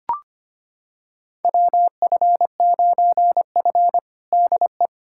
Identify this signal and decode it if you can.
Morse code, heard as a keyed tone, about 25 wpm, 700 Hz. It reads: WF9F DE
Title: USA Callsign DE Callsign - 4 Characters - 25wpm